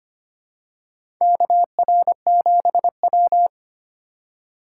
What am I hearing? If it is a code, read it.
Morse code, heard as a keyed tone, about 25 wpm, 700 Hz. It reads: KR7W
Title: K1USN SST QSO Practice - 25wpm